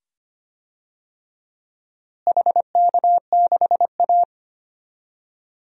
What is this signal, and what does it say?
Morse code, heard as a keyed tone, about 25 wpm, 700 Hz. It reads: HK6A
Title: DXCC-100 Call Signs - 4 Characters - Repeated 2x - 25wpm